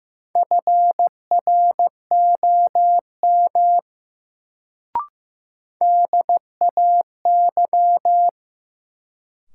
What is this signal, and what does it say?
Morse code, heard as a keyed tone, about 15 wpm, 700 Hz. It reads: FROM DAY
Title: Sets of 2 Words - New Word Drill - 15wpm